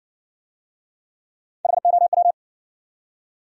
Morse code, heard as a keyed tone, about 60 wpm, 700 Hz. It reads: HOW